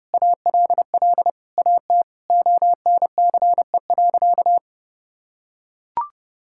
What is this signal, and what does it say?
Morse code, heard as a keyed tone, about 30 wpm, 700 Hz. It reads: ALL AT ONCE.
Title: Sentences from Top 1000 Words - Review Part 5 - 30wpm